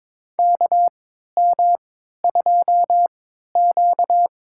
Morse code, heard as a keyed tone, about 22 wpm, spaced as for 12 wpm, 700 Hz. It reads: KM2Q
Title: USA Call Signs - 4 Characters - 3x Character Spacing - 22wpm